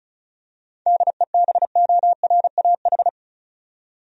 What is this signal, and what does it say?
Morse code, heard as a keyed tone, about 36 wpm, 700 Hz. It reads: DEBORAH
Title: USA Common Names - 35wpm